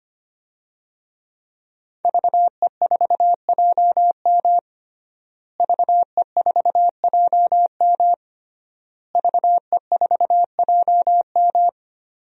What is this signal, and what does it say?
Morse code, heard as a keyed tone, about 25 wpm, 700 Hz. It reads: VE4JM VE4JM VE4JM
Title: DXCC-100 Call Signs - 5 Characters - Repeated 3x - 25wpm